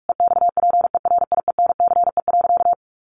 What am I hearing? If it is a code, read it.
Morse code, heard as a keyed tone, about 44 wpm, 700 Hz. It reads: EXPERIENCE.
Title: Sentences from Top 600 Words 45wpm